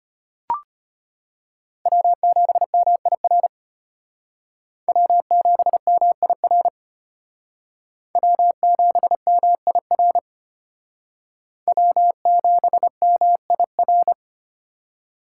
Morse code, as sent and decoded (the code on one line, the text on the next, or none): .-- --... -- .. .-. / .-- --... -- .. .-. / .-- --... -- .. .-. / .-- --... -- .. .-.
W7MIR W7MIR W7MIR W7MIR